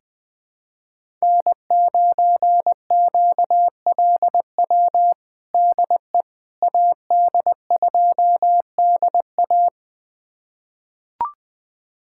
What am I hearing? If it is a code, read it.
Morse code, heard as a keyed tone, about 20 wpm, 700 Hz. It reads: N9QLW DE AD2DA
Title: USA Callsign DE Callsign - 5 Characters - 20wpm